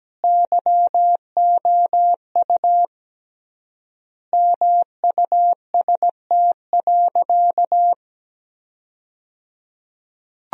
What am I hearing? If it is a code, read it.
Morse code, heard as a keyed tone, about 17 wpm, 700 Hz. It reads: YOU MUST.
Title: 3x Word-Spacing - Sentences from Top 200 Words - 17wpm